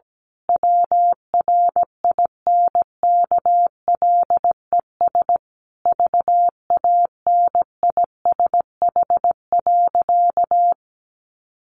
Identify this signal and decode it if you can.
Morse code, heard as a keyed tone, about 17 wpm, 700 Hz. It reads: WRINKLES VANISH.
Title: A Sound Of Thunder 17wpm